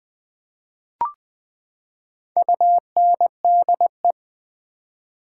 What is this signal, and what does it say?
Morse code, heard as a keyed tone, about 20 wpm, 700 Hz. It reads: UNDE
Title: Build A Word - 4 Letters At A Time - 10-Letter Words - 20wpm